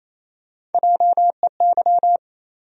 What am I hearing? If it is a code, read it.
Morse code, heard as a keyed tone, about 28 wpm, 700 Hz. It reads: JEY
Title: USA Call Sign Postfixes - 3 Letters 28wpm